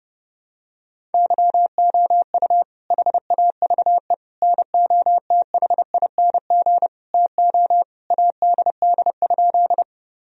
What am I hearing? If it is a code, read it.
Morse code, heard as a keyed tone, about 30 wpm, 700 Hz. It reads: YOU HAVE NOTHING TO ADD?
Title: Sentences from Top 1000 Words - Review Part 1 - 30wpm